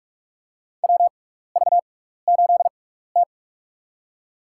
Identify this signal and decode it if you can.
Morse code, heard as a keyed tone, about 44 wpm, spaced as for 15 wpm, 700 Hz. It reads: WV8T
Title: USA Call Signs - 4 Characters - 6x Character Spacing - 45wpm